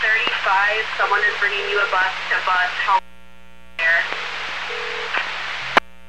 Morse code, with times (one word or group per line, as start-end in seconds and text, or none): none